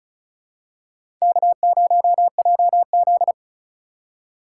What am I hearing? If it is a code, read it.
Morse code, heard as a keyed tone, about 34 wpm, 700 Hz. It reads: K0JZ
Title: USA Call Signs - 4 Characters - 2x Character Spacing - 35wpm